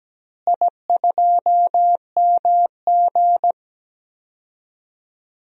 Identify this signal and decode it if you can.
Morse code, heard as a keyed tone, about 17 wpm, 700 Hz. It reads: I2MG
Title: DXCC-100 Call Signs - 4 Characters - Repeated 2x - 17wpm